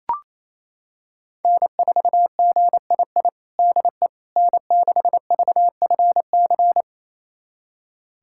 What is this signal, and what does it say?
Morse code, heard as a keyed tone, about 27 wpm, 700 Hz. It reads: N4GII DE N6VFC
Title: USA Callsign DE Callsign - 5 Characters - 28wpm